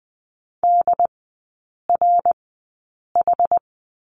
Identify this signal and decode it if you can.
Morse code, heard as a keyed tone, about 20 wpm, spaced as for 8 wpm, 690 Hz. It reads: DRH
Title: 081 Lesson 14e - Number 2 - Three Characters 20wpm